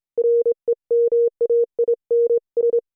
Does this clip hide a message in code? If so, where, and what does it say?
Yes: Morse code from 0.2 to 2.8 s, heard as a keyed tone, about 28 wpm, 470 Hz. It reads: REMAINS